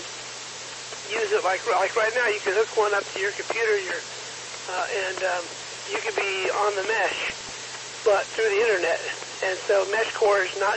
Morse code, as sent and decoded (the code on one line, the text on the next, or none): none